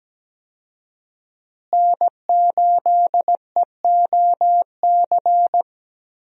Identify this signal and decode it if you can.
Morse code, heard as a keyed tone, about 17 wpm, 700 Hz. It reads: N8EOC
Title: USA Call Signs - 5 Characters - Repeated 2x - 17wpm